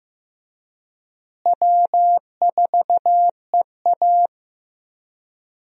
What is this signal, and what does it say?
Morse code, heard as a keyed tone, about 15 wpm, 700 Hz. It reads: W4EA
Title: USA Call Signs - 4 Characters - Repeated 3x - 15wpm